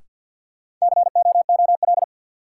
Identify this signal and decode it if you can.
Morse code, heard as a keyed tone, about 48 wpm, 700 Hz. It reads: COOL
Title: Sentences from Top 1200 Words - New Word Drill Alt - 50wpm